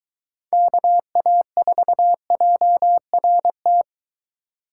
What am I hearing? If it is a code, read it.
Morse code, heard as a keyed tone, about 23 wpm, 700 Hz. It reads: KA4JRT